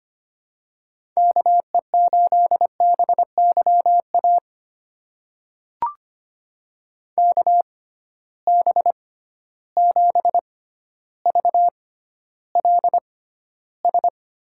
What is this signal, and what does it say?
Morse code, heard as a keyed tone, about 25 wpm, spaced as for 13 wpm, 700 Hz. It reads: KE8BYA KB7VLS
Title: USA Call Signs - 6 Characters - 6x Character Spacing - 25wpm